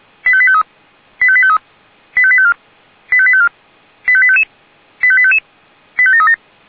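Five-tone selective call (ZVEI), heard as quick runs of five tones, tones of about 70 ms. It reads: 85853 85853 85854 85854 85855 85855 85737
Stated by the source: ZVEI five-tone selective call received on SDR.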